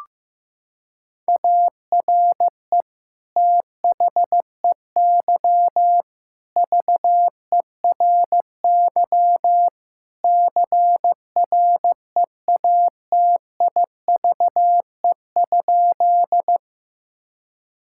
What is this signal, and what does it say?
Morse code, heard as a keyed tone, about 16 wpm, 700 Hz. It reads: ARE THEY VERY CREATIVE?
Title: Sentences from Top 1000 Words - Review Part 5 - 15wpm